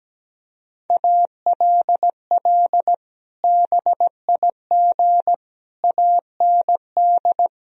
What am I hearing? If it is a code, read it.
Morse code, heard as a keyed tone, about 17 wpm, 700 Hz. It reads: ALL BIG AND
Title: Sentences from Top 1000 Words - Review Part 1 - 17wpm